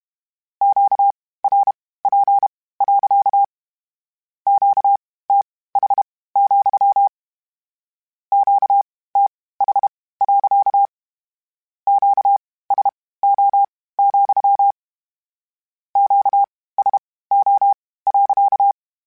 Morse code, heard as a keyed tone, about 32 wpm, spaced as for 18 wpm, 800 Hz. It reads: QRP. QTH, QTH. QSO, QSO.
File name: Q Calls / Q Calls-mp3s / Q Calls sppc @30